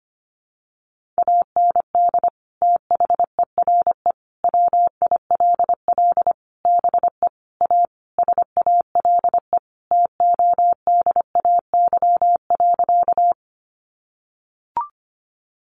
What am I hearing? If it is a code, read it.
Morse code, heard as a keyed tone, about 25 wpm, 700 Hz. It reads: AND THERE WILL BE A SALE TODAY.
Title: Sentences from Top 1400 Words 25wpm